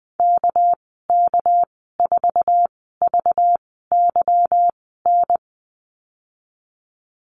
Morse code, heard as a keyed tone, about 20 wpm, spaced as for 14 wpm, 700 Hz. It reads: KK4VYN
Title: 219 Lesson 35g - Number 0 - Call Signs 20wpm